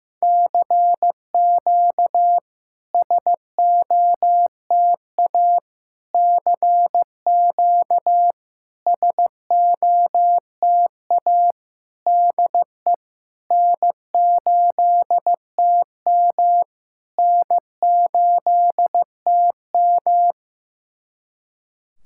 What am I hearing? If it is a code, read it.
Morse code, heard as a keyed tone, about 16 wpm, 700 Hz. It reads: CQ SOTA CQ SOTA DE N8TM N8TM
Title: SOTA QSO Practice 15wpm